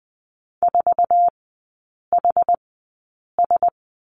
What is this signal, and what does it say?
Morse code, heard as a keyed tone, about 20 wpm, spaced as for 8 wpm, 700 Hz. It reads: 4HS
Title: 060 Lesson 11e - Letter h - Three Characters 20wpm